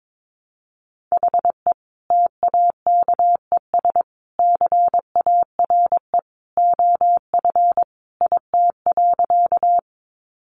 Morse code, heard as a keyed tone, about 22 wpm, 700 Hz. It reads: HE TAKES CARE OF IT.
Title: Additional Easy Phrases 22wpm